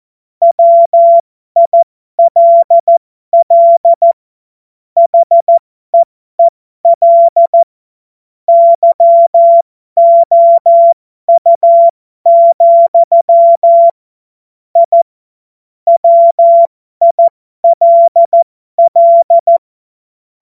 Morse code, heard as a keyed tone, about 14 wpm, spaced as for 12 wpm, 680 Hz. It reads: WILL HEEL YOU, I WILL
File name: Morsekurs / Witze